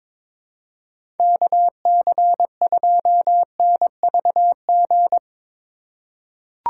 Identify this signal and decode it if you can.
Morse code, heard as a keyed tone, about 21 wpm, 700 Hz. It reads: KC2NVG
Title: USA Call Signs - 4x Character Spacing - 22wpm